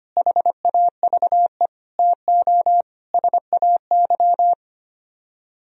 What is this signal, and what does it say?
Morse code, heard as a keyed tone, about 30 wpm, 700 Hz. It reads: HAVE TO SAY
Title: Speed-Racing - Sets of 4 Words 25wpm